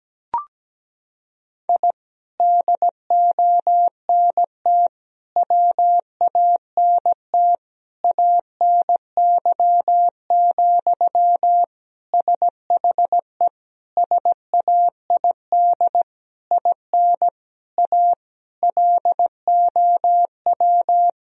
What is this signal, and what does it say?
Morse code, heard as a keyed tone, about 17 wpm, 700 Hz. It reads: I DONT WANT ANY, SHE SAID IN A LOW
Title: Sentences from Top 1000 Words - Review Part 1 - 17wpm